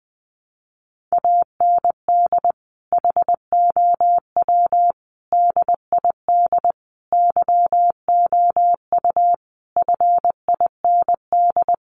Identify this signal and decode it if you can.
Morse code, heard as a keyed tone, about 19 wpm, 700 Hz. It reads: AND HOW DID YOU FIND
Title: Sentences from Top 200 Words 20wpm